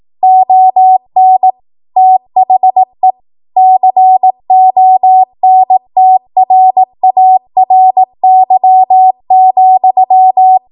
Morse code, heard as a keyed tone, about 18 wpm, 750 Hz. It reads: ON THE CONTRARY,
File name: Archive / 18 WPM / 18WPM